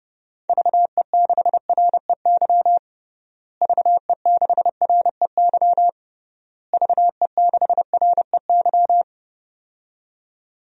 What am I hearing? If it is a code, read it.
Morse code, heard as a keyed tone, about 30 wpm, 700 Hz. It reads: VE6REY VE6REY VE6REY
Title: DXCC-100 Call Signs - 6 Characters - Repeated 3x - 30wpm